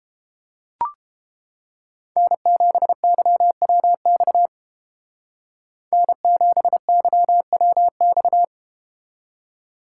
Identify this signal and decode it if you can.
Morse code, heard as a keyed tone, about 30 wpm, 700 Hz. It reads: N7YWX N7YWX